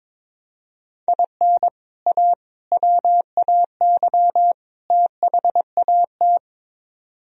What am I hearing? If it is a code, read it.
Morse code, heard as a keyed tone, about 22 wpm, 700 Hz. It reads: IN A WAY THAT
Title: Speed-Racing - Sets of 4 Words 22wpm